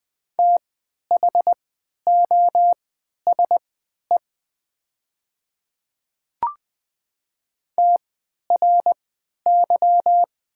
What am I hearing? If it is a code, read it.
Morse code, heard as a keyed tone, about 20 wpm, spaced as for 11 wpm, 700 Hz. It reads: THOSE TRY
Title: 3x Character Spacing - Top 300 Words - 20wpm